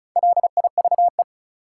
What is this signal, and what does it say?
Morse code, heard as a keyed tone, about 36 wpm, 700 Hz. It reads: LIVE
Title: Sentences from Top 200 Words - New Word Drill - 35wpm